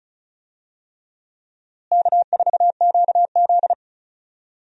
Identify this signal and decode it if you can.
Morse code, heard as a keyed tone, about 36 wpm, 700 Hz. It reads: K4QZ
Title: USA Call Signs - 4 Characters - Repeated 3x - 35wpm